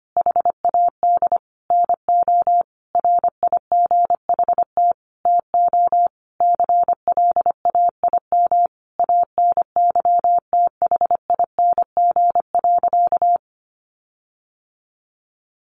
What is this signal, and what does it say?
Morse code, heard as a keyed tone, about 25 wpm, 700 Hz. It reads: HAD NO RIGHT TO CLAIM ANYTHING.
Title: Sentences from Top 1500 Words 25wpm